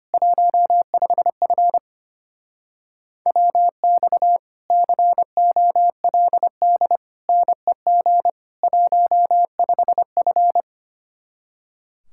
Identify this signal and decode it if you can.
Morse code, heard as a keyed tone, about 29 wpm, 700 Hz. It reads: 15F WX COLD NEG 15F
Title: Speed-Racing Common QSO Phrases - 25wpm